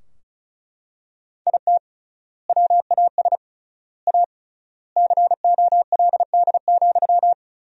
Morse code, heard as a keyed tone, about 36 wpm, 700 Hz. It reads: IT WAS A COLD,
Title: 3x Word-Spacing - Sentences from Top 500 Words - 35wpm